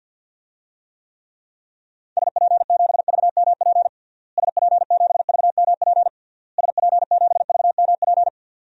Morse code, heard as a keyed tone, about 50 wpm, 700 Hz. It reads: SP7VMP SP7VMP SP7VMP
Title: DXCC-100 Call Signs - 6 Characters - Repeated 3x - 50wpm